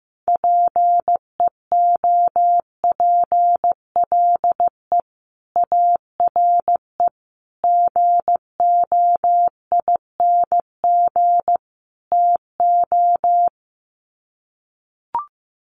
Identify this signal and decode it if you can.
Morse code, heard as a keyed tone, about 15 wpm, 700 Hz. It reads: PEOPLE ARE GOING TO
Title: Sets of 4 Words 15wpm